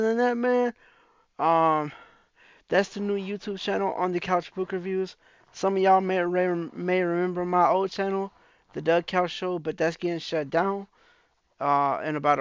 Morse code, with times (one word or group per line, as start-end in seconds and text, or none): none